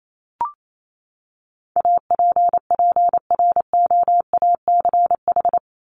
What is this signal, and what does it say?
Morse code, heard as a keyed tone, about 28 wpm, 700 Hz. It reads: APPROACH
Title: Top 751-1000 Words 28wpm